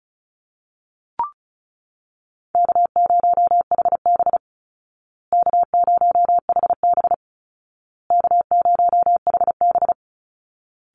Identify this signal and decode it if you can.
Morse code, heard as a keyed tone, about 34 wpm, 700 Hz. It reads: K0HB K0HB K0HB
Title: Easy Call Signs Repeated 3x - 35wpm